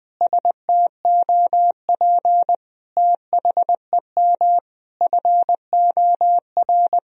Morse code, heard as a keyed tone, about 20 wpm, 700 Hz. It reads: STOP THEM FOR
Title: Sentences from Top 1000 Words - Review Part 1 - 20wpm